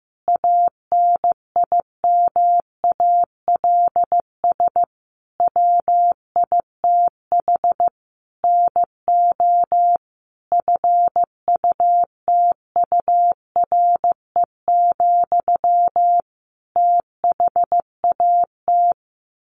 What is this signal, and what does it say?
Morse code, heard as a keyed tone, about 14 wpm, 700 Hz. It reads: ANIMALS WITH NO FUTURE, THAT
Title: A Sound Of Thunder 15wpm